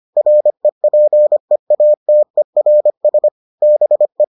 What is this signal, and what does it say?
Morse code, heard as a keyed tone, about 25 wpm, 600 Hz. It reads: REPEATERS BE